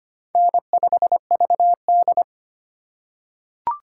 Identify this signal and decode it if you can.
Morse code, heard as a keyed tone, about 25 wpm, 700 Hz. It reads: N5VD
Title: USA Call Signs - 4x Character Spacing - 25wpm